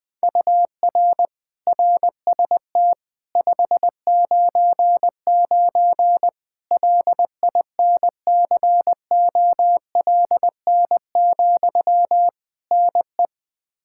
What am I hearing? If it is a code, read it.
Morse code, heard as a keyed tone, about 20 wpm, 700 Hz. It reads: UR RST 599 LINCOLN, NE
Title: Practice QSO - Exchange 1 - Beginning - 20wpm